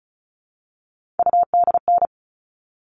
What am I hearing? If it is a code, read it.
Morse code, heard as a keyed tone, about 34 wpm, 700 Hz. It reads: UDN
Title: All Cut Numbers - 3-Digit 35wpm